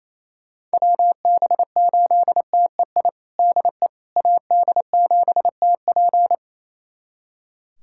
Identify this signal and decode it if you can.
Morse code, heard as a keyed tone, about 28 wpm, 700 Hz. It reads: WB8TEI DE AD7TP
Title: USA Callsign DE Callsign 28wpm